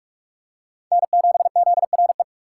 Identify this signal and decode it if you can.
Morse code, heard as a keyed tone, about 46 wpm, 700 Hz. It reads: N7ZRE
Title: USA Call Signs - 5 Characters - 3x Character Spacing - 45wpm